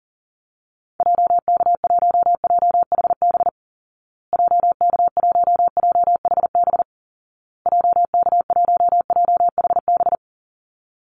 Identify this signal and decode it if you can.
Morse code, heard as a keyed tone, about 37 wpm, 700 Hz. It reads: JK1JHB JK1JHB JK1JHB
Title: Call Signs Repeated 2x - 40wpm